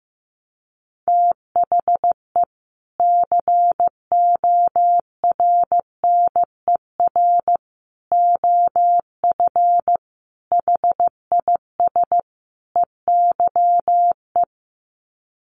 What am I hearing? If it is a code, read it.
Morse code, heard as a keyed tone, about 15 wpm, 700 Hz. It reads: THE CORNER OF HIS EYE
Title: Sets of 5 Words Encore2 15wpm